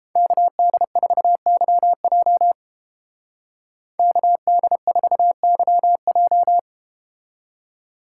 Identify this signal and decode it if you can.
Morse code, heard as a keyed tone, about 31 wpm, 700 Hz. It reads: KD4YJ KD4YJ